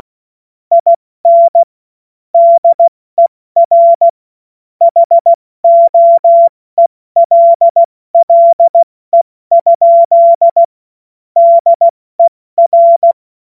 Morse code, heard as a keyed tone, about 16 wpm, 680 Hz. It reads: IN DER HOELLE? DER
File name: Morsekurs / Witze / Witz 30. August 2022 70 F80